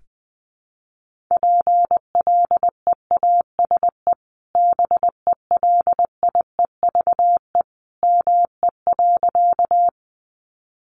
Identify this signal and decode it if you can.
Morse code, heard as a keyed tone, about 20 wpm, 700 Hz. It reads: PLEASE BELIEVE ME.
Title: Sentences from Top 1500 Words 20wpm